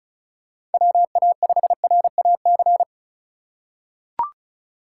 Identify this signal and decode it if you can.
Morse code, heard as a keyed tone, about 34 wpm, 700 Hz. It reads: WA5RAC
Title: USA Call Signs - 6 Characters - 3x Character Spacing - 35wpm